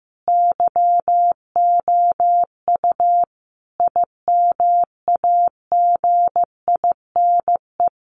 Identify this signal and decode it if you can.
Morse code, heard as a keyed tone, about 15 wpm, 700 Hz. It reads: YOU IMAGINE
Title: Sentences from Top 1400 Words 15wpm